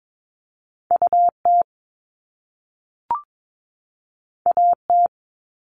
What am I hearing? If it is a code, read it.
Morse code, heard as a keyed tone, about 22 wpm, 700 Hz. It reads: UT AT
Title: Top Two Letter Combinations in Words 22wpm